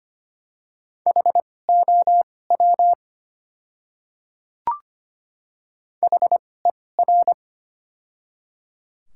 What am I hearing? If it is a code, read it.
Morse code, heard as a keyed tone, about 25 wpm, spaced as for 18 wpm, 700 Hz. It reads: HOW HER
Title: 2x Character Spacing - Top 200 Words - 25wpm